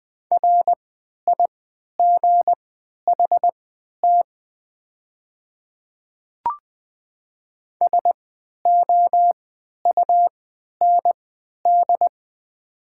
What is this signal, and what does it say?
Morse code, heard as a keyed tone, about 20 wpm, spaced as for 11 wpm, 690 Hz. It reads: RIGHT SOUND
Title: 3x Character Spacing - Top 500 Words - 20wpm